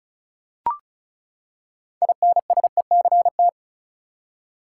Morse code, heard as a keyed tone, about 36 wpm, 700 Hz. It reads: INSECT
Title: Six Letter Words - 35wpm